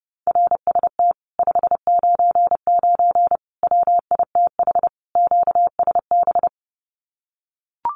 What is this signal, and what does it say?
Morse code, heard as a keyed tone, about 30 wpm, 700 Hz. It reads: RST 599 WITH QSB
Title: Common QSO Phrases Encore 30wpm